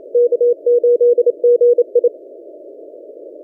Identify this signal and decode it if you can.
Morse code, heard as a keyed tone, about 26 wpm, 480 Hz. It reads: K8GI
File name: posts / v63mj-17cw- -0006z